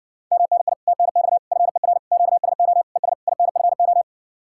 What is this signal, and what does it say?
Morse code, heard as a keyed tone, about 60 wpm, 700 Hz. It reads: MNI TNX FER QSO ES INFO